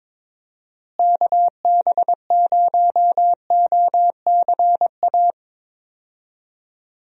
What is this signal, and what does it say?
Morse code, heard as a keyed tone, about 22 wpm, 700 Hz. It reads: KB0OCA
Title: USA Call Signs - 6 Characters - Repeated 2x - 22wpm